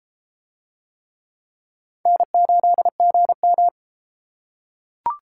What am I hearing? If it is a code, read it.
Morse code, heard as a keyed tone, about 31 wpm, 700 Hz. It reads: N8GM